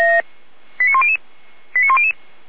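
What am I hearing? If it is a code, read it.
Five-tone selective call (ZVEI), heard as quick runs of five tones, tones of about 70 ms. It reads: 89100 89100
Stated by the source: ZVEI five-tone selective call received by SDR, from the Feuerwehr Orlamünde audio server.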